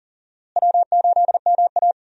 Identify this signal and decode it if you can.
Morse code, heard as a keyed tone, about 40 wpm, 700 Hz. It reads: W8MA
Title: USA Call Signs - 4 Characters - Repeated 2x - 40wpm